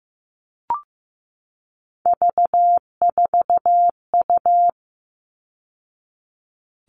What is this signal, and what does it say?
Morse code, heard as a keyed tone, about 15 wpm, 700 Hz. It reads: V4U
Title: All Cut Numbers - 3-Digit 15wpm